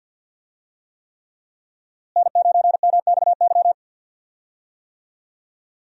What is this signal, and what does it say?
Morse code, heard as a keyed tone, about 49 wpm, 690 Hz. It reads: N9MXY
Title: USA Call Signs - 5 Characters - Repeated 3x - 50wpm